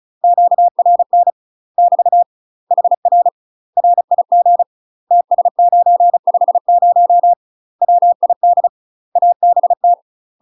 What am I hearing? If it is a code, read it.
Morse code, heard as a keyed tone, about 34 wpm, 700 Hz. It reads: QRN <BT> HR RIG TS950 WID ABT
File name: UK MASTER QSOs TRAINING / 35WPM / UK MASTER QSOs 0024